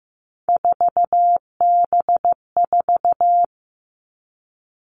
Morse code, heard as a keyed tone, about 15 wpm, 700 Hz. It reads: 4B4
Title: Three Challenging Letter-Number Combinations 15wpm